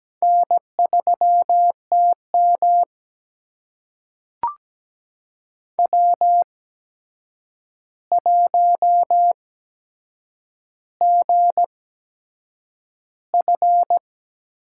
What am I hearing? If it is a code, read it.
Morse code, heard as a keyed tone, about 17 wpm, spaced as for 7 wpm, 700 Hz. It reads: N3TM W1GF
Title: USA Call Signs - 4 Characters - 8x Character Spacing - 17wpm